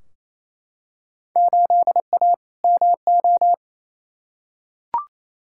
Morse code, heard as a keyed tone, about 28 wpm, 700 Hz. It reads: 8A MO
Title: Field Day Exchanges Encore - 28wpm